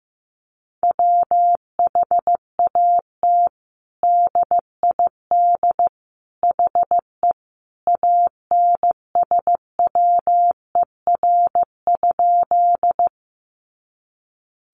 Morse code, heard as a keyed tone, about 15 wpm, 700 Hz. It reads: WHAT DID HE ANSWER?
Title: Sentences from Top 300 Words 15wpm